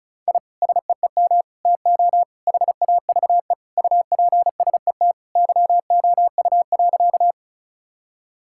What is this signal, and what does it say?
Morse code, heard as a keyed tone, about 34 wpm, 700 Hz. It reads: I SEEM TO HAVE UPSET YOU.
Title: Sentences from Top 1000 Words - Review Part 5 - 35wpm